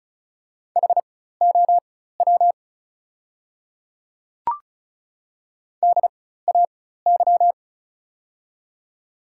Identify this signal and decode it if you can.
Morse code, heard as a keyed tone, about 36 wpm, spaced as for 16 wpm, 700 Hz. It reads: HOW DAY
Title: 4x Character Spacing - Top 100 Words - 35wpm